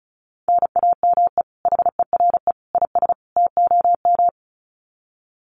Morse code, heard as a keyed tone, about 34 wpm, 700 Hz. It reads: NAME HERE IS TOM
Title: Common QSO Phrases 35wpm